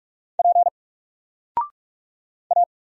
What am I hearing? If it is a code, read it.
Morse code, heard as a keyed tone, about 46 wpm, 700 Hz. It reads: P A
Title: Single Letter-Number 45wpm